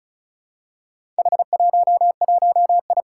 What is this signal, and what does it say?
Morse code, heard as a keyed tone, about 36 wpm, 700 Hz. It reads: H11I
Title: Four Challenging Letter-Number Combinations 35wpm